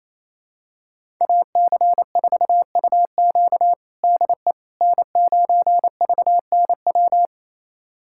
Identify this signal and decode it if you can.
Morse code, heard as a keyed tone, about 28 wpm, 700 Hz. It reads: AC4UQ DE N9VNW
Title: USA Callsign DE Callsign - 5 Characters - 28wpm